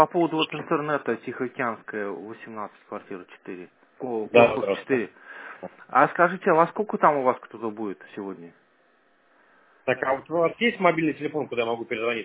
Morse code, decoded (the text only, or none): none